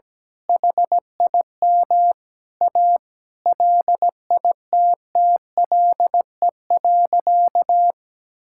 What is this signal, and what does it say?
Morse code, heard as a keyed tone, about 17 wpm, 700 Hz. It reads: HIM A LITTLE.
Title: Sentences from Top 1000 Words - Review Part 6 - 17wpm